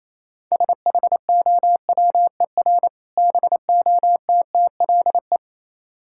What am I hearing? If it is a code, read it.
Morse code, heard as a keyed tone, about 28 wpm, 700 Hz. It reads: SHOWER BOTTLE